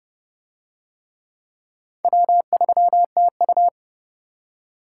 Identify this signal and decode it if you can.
Morse code, heard as a keyed tone, about 29 wpm, 700 Hz. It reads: W3TU